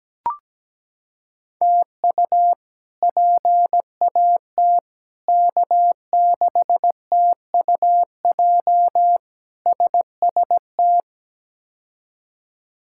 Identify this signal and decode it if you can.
Morse code, heard as a keyed tone, about 17 wpm, 700 Hz. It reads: TU PAT K6TUJ SST
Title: K1USN SST QSO Practice - 17wpm